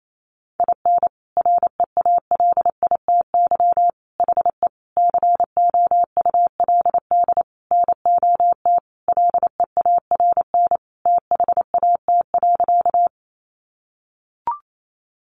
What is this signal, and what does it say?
Morse code, heard as a keyed tone, about 28 wpm, 700 Hz. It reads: IN REALITY HE COULD NOT LEARN THAT.
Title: Sentences from Top 1200 Words 28wpm